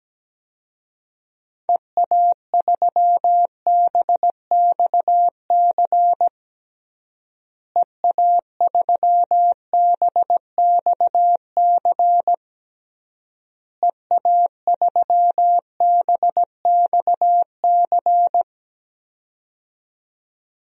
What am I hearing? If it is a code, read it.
Morse code, heard as a keyed tone, about 17 wpm, 700 Hz. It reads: EA3BXC EA3BXC EA3BXC
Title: DXCC-100 Call Signs - 6 Characters - Repeated 3x - 17wpm